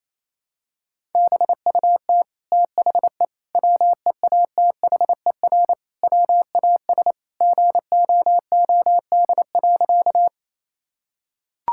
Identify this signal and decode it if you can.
Morse code, heard as a keyed tone, about 28 wpm, 700 Hz. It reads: BUT THE WEATHER WAS GOOD.
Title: Sentences from Top 1000 Words - Review Part 5 - 28wpm